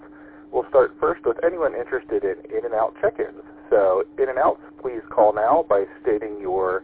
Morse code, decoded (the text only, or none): none